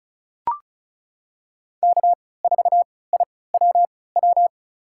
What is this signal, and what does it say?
Morse code, heard as a keyed tone, about 36 wpm, spaced as for 20 wpm, 700 Hz. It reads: K4IWW
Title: USA Call Signs - 5 Characters - 3x Character Spacing - 35wpm